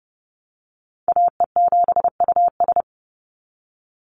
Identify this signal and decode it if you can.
Morse code, heard as a keyed tone, about 30 wpm, 700 Hz. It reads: AE7US
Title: Call Signs Repeated 3x - 30wpm